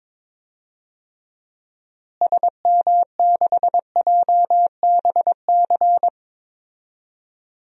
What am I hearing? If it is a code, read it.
Morse code, heard as a keyed tone, about 22 wpm, 700 Hz. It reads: SM6JBC
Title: DXCC-100 Call Signs - 6 Characters - Repeated 2x - 22wpm